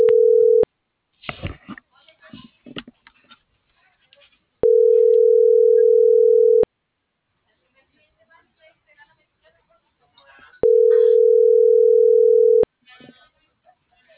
Ringback tone (North America), 440 + 480 Hz.